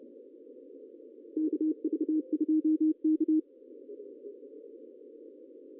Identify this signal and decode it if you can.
Morse code, heard as a keyed tone, about 30 wpm, 310 Hz. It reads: KV2K
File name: CW / cqww-cw-k5tr